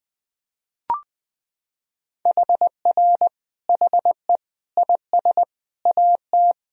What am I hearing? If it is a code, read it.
Morse code, heard as a keyed tone, about 20 wpm, 700 Hz. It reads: HR HE IS AT
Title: Sentences from Top 300 Words and Abbreviations 20wpm